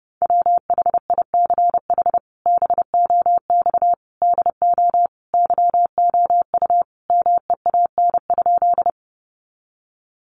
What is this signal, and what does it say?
Morse code, heard as a keyed tone, about 30 wpm, 700 Hz. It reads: WHICH BOX DO YOU MEAN?
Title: Sentences from Top 500 Words 30wpm